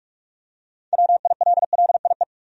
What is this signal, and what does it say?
Morse code, heard as a keyed tone, about 46 wpm, 700 Hz. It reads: WILLIE
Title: USA Common Names - 45wpm